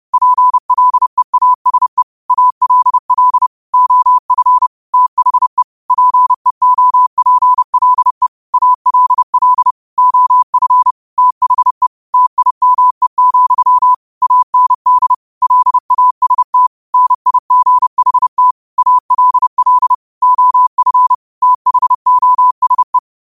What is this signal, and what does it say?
Morse code, heard as a keyed tone, about 30 wpm, 1000 Hz. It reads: PLEASE ALL OF THE PEOPLE ALL OF THE TIME, AND LAST NIGHT ALL OF THOSE
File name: qotdfiles / 30wpm